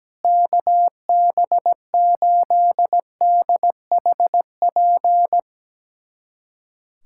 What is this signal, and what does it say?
Morse code, heard as a keyed tone, about 17 wpm, 700 Hz. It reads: KB8DHP
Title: USA Callsign DE Callsign 17wpm